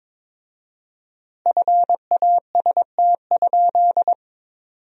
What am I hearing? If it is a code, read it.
Morse code, heard as a keyed tone, about 22 wpm, 700 Hz. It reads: FAST?
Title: 4x Word-Spacing - Sentences from Top 400 Words - 22wpm